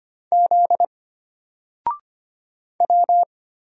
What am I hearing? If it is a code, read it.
Morse code, heard as a keyed tone, about 25 wpm, 700 Hz. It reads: Z W
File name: Single-Letter-Number-25wpm